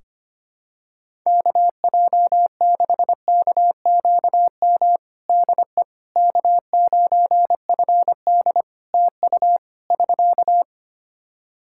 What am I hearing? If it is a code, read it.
Morse code, heard as a keyed tone, about 25 wpm, 700 Hz. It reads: KJ6KQM DE K9FD TU <SK>
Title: K1USN SST QSO Practice 25wpm